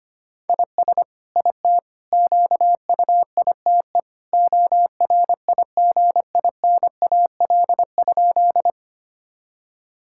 Morse code, heard as a keyed tone, about 25 wpm, 700 Hz. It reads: IS IT QUITE ORIGINAL?
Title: Sentences from Top 1000 Words - Review Part 6 - 25wpm